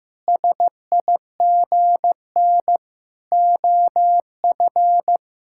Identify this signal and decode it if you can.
Morse code, heard as a keyed tone, about 15 wpm, 700 Hz. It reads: SIGN OF
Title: Sentences from Top 1000 Words - Review Part 6 - 15wpm